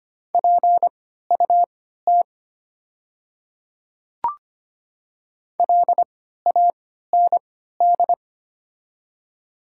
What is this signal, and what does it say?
Morse code, heard as a keyed tone, about 25 wpm, spaced as for 14 wpm, 700 Hz. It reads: PUT LAND
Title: 3x Character Spacing - Top 300 Words - 25wpm